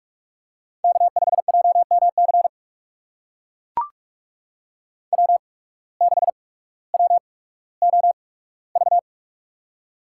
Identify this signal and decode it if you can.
Morse code, heard as a keyed tone, about 44 wpm, spaced as for 19 wpm, 700 Hz. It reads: K5JMC W6WOV
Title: USA Call Signs - 5 Characters - 8x Character Spacing - 45wpm